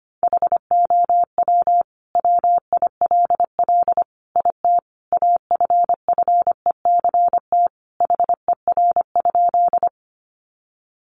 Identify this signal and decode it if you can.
Morse code, heard as a keyed tone, about 25 wpm, 700 Hz. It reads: HOW WILL IT AFFECT HER?
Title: Sentences from Top 900 Words 25wpm